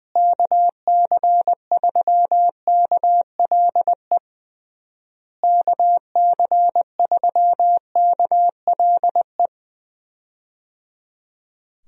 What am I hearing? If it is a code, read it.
Morse code, heard as a keyed tone, about 21 wpm, 700 Hz. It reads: KC3KLE KC3KLE
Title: USA Call Signs - 6 Characters - Repeated 3x - 20wpm